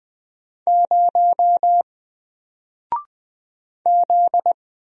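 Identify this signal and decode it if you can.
Morse code, heard as a keyed tone, about 20 wpm, 700 Hz. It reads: 0 Z
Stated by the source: Single letter and number Morse drill at 20 wpm, from N4LCD.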